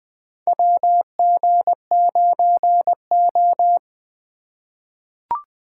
Morse code, heard as a keyed tone, about 20 wpm, 700 Hz. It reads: WG9O
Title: USA Call Signs - 4 Characters - 4x Character Spacing - 20wpm